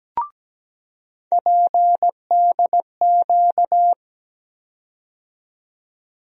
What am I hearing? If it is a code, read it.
Morse code, heard as a keyed tone, about 17 wpm, 700 Hz. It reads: PDQ
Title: USA Call Sign Postfixes - 3 Letters 17wpm